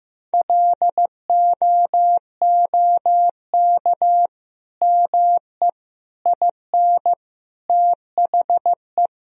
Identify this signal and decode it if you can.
Morse code, heard as a keyed tone, about 15 wpm, 700 Hz. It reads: LOOK ME IN THE
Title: Sentences from Top 1000 Words - Review Part 2 - 15wpm